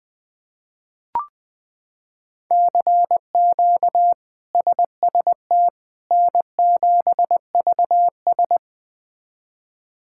Morse code, heard as a keyed tone, about 20 wpm, 700 Hz. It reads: CQ SST N7VS
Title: K1USN SST QSO Practice - 20wpm